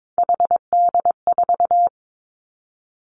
Morse code, heard as a keyed tone, about 22 wpm, 700 Hz. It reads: HD4
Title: Three Challenging Letter-Number Combinations 22wpm